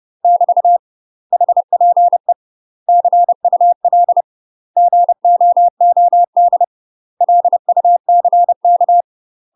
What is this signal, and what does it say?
Morse code, heard as a keyed tone, about 30 wpm, 700 Hz. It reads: <BT> HPE CUL GOOD LUCK
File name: UK MASTER QSOs TRAINING / 30WPM / UK MASTER QSOs 0044